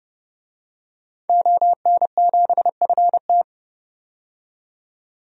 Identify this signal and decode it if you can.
Morse code, heard as a keyed tone, about 30 wpm, 690 Hz. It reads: ON7FT
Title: DXCC-100 Call Signs - 5 Characters - 30wpm